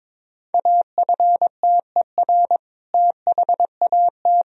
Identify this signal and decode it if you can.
Morse code, heard as a keyed tone, about 22 wpm, 700 Hz. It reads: AFTER THAT
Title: Sentences from Top 1000 Words - Review Part 1 - 22wpm